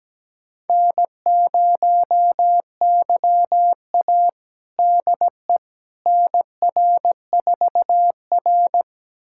Morse code, heard as a keyed tone, about 17 wpm, 700 Hz. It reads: N0YA DE NR4R
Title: USA Callsign DE Callsign - 4 Characters - 17wpm